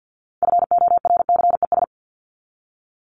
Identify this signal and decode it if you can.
Morse code, heard as a keyed tone, about 50 wpm, 700 Hz. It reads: FORCES
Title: ES:S Suffix - Top 100 Words with Base Words 50wpm